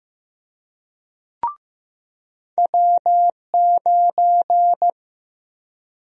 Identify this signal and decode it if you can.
Morse code, heard as a keyed tone, about 15 wpm, 700 Hz. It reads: W9
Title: USA Call Sign Prefixes - 15wpm